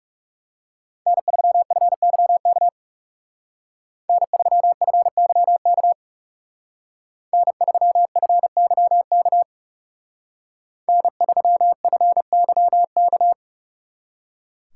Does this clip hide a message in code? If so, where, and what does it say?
Yes: Morse code from 1.1 to 13.3 s, heard as a keyed tone, about 39 wpm, 700 Hz. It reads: N3FYK N3FYK N3FYK N3FYK